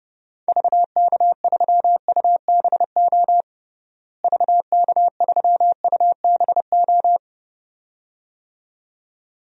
Morse code, heard as a keyed tone, about 32 wpm, 700 Hz. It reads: VK3UBO VK3UBO
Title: DXCC-100 Call Signs - 6 Characters - Repeated 2x - 30wpm